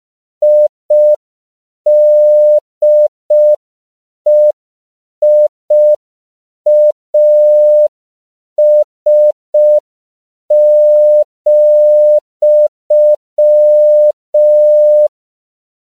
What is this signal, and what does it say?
Morse code, heard as a keyed tone, about 5 wpm, 600 Hz. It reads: IDEIAS,
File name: CW / 5PPM